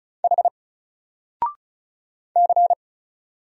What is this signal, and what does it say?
Morse code, heard as a keyed tone, about 36 wpm, 700 Hz. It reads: H C
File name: Single-Letter-Number-35wpm